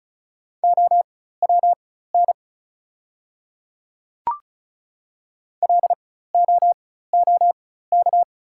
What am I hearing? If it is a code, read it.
Morse code, heard as a keyed tone, about 36 wpm, spaced as for 16 wpm, 700 Hz. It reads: OWN LOOK
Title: 4x Character Spacing - Top 300 Words - 35wpm